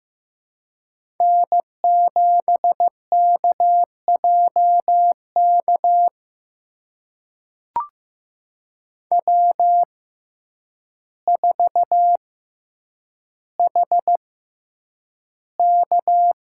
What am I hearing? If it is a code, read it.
Morse code, heard as a keyed tone, about 15 wpm, spaced as for 8 wpm, 700 Hz. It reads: N7KJK W4HK
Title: USA Call Signs - 6x Character Spacing - 15wpm